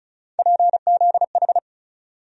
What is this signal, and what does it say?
Morse code, heard as a keyed tone, about 36 wpm, 700 Hz. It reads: PZH
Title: USA Call Sign Postfixes - 3 Letters 35wpm